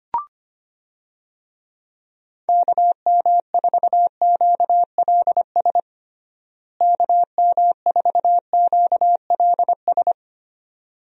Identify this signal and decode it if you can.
Morse code, heard as a keyed tone, about 25 wpm, 700 Hz. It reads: KM4QLS KM4QLS
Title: USA Call Signs - 6 Characters - Repeated 2x - 25wpm